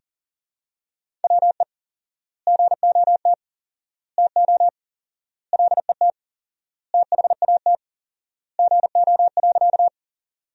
Morse code, heard as a keyed tone, about 40 wpm, 700 Hz. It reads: WE GOT TO LET THAT GO.
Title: 4x Word-Spacing - Sentences from Top 300 Words - 40wpm